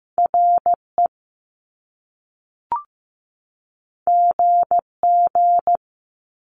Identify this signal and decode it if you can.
Morse code, heard as a keyed tone, about 15 wpm, 700 Hz. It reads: RE GG
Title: Top Two Letter Combinations in Words 15wpm